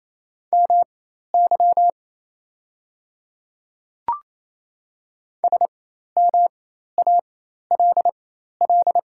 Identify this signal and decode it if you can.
Morse code, heard as a keyed tone, about 28 wpm, spaced as for 13 wpm, 690 Hz. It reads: MY SMALL
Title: 4x Character Spacing - Top 200 Words - 28wpm